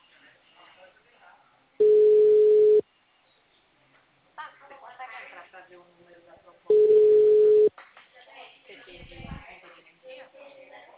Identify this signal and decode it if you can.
Ringback tone (Europe, 425 Hz), with speech.